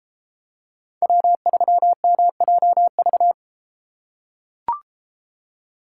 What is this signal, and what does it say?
Morse code, heard as a keyed tone, about 33 wpm, 700 Hz. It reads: W3MJV